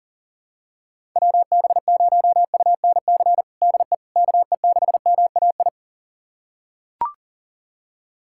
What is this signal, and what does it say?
Morse code, heard as a keyed tone, about 40 wpm, 700 Hz. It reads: WB0UNC DE KE6MAI
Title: USA Callsign DE Callsign 40wpm